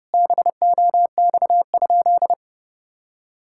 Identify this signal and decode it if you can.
Morse code, heard as a keyed tone, about 30 wpm, 700 Hz. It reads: BOX?
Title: Sentences from Top 1000 Words - Review Part 5 - 30wpm